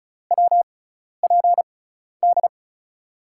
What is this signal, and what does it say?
Morse code, heard as a keyed tone, about 36 wpm, spaced as for 12 wpm, 700 Hz. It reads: WPD
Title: USA Call Signs - 6 Characters - 6x Character Spacing - 35wpm